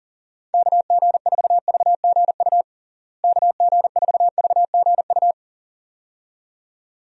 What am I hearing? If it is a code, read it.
Morse code, heard as a keyed tone, about 40 wpm, 700 Hz. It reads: KG4VGU KG4VGU
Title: USA Call Signs - 6 Characters - Repeated 2x - 40wpm